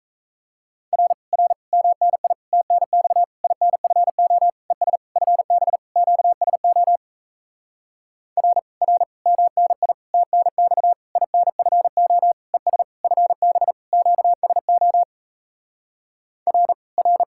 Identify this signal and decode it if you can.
Morse code, heard as a keyed tone, about 37 wpm, 700 Hz. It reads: R R MNI TNX INFO ES FB QSO R R MNI TNX INFO ES FB QSO R R